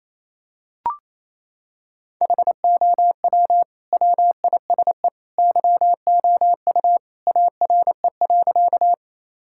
Morse code, heard as a keyed tone, about 28 wpm, 700 Hz. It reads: HOW WISE YOU ARE.
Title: Sentences from Top 1600 Words 28wpm